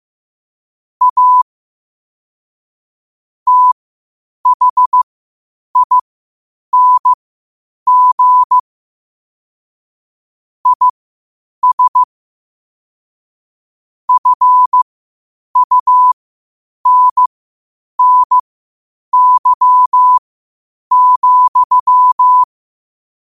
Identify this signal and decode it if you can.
Morse code, heard as a keyed tone, about 15 wpm, spaced as for 8 wpm, 1000 Hz. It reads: A THING IS FUNNY,
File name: qotdfiles / 8wpm